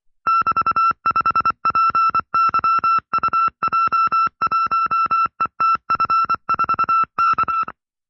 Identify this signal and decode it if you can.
Morse code, heard as a keyed tone, about 24 wpm, 1340 Hz. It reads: <BT>5PYUJ1ETF4K
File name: FreeFiles / CW-practice / 20WPM-side2